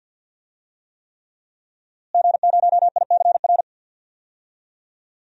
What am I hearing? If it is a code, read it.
Morse code, heard as a keyed tone, about 52 wpm, 700 Hz. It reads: G0ICR
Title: DXCC-100 Call Signs - 5 Characters - Repeated 3x - 50wpm